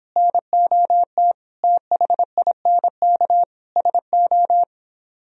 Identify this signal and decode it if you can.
Morse code, heard as a keyed tone, about 25 wpm, 700 Hz. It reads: NOT THINK SO